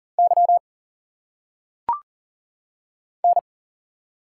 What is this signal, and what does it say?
Morse code, heard as a keyed tone, about 40 wpm, 700 Hz. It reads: Y N
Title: Single Letters - Phonetic Pronunciation 40wpm